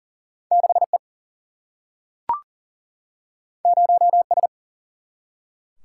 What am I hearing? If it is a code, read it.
Morse code, heard as a keyed tone, about 38 wpm, 700 Hz. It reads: 6E 0S
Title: Single-Letter Single-Number 40wpm